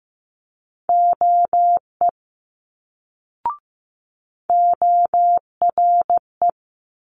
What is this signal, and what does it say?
Morse code, heard as a keyed tone, about 15 wpm, 700 Hz. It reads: OE ORE
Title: Multi-letter Phonemes Quick Fire 15wpm